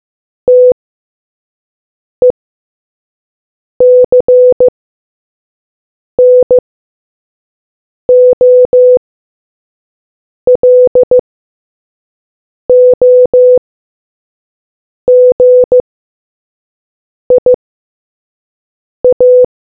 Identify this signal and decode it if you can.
Morse code, heard as a keyed tone, about 15 wpm, spaced as for 5 wpm, 500 Hz. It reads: TECNOLOGIA